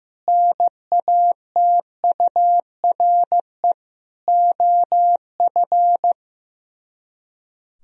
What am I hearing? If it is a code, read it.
Morse code, heard as a keyed tone, about 15 wpm, 700 Hz. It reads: NATURE OF
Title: Speed-Racing - Sets of 3 Words 15wpm